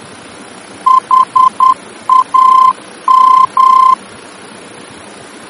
Morse code, heard as a keyed tone, about 10 wpm, 1040 Hz. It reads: HAM
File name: germany / mx / HAM-D-VOR